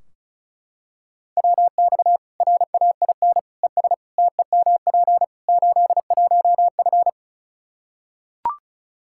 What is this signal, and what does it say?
Morse code, heard as a keyed tone, about 34 wpm, 700 Hz. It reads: WX RAIN ES TEMP 81F
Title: Practice QSO - Exchange 2 - Info - 35wpm